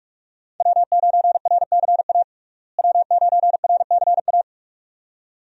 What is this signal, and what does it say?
Morse code, heard as a keyed tone, about 44 wpm, 700 Hz. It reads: W9RCA W9RCA
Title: USA Call Signs - 5 Characters - Repeated 2x - 45wpm